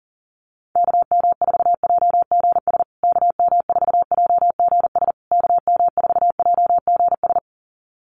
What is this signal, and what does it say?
Morse code, heard as a keyed tone, about 40 wpm, 700 Hz. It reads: KM4JGS KM4JGS KM4JGS
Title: USA Call Signs Repeated 3x 40wpm